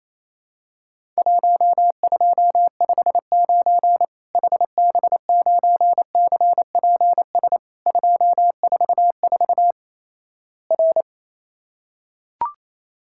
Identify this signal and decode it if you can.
Morse code, heard as a keyed tone, about 28 wpm, 700 Hz. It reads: 1259 HB9CPS 244 R
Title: Worked All Europe DX Contest - QTC Exchange - 28wpm